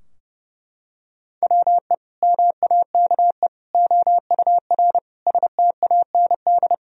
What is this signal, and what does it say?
Morse code, heard as a keyed tone, about 30 wpm, 700 Hz. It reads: WE MAKE OUR STAND
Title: Sentences from Top 1000 Words - Review Part 2 - 30wpm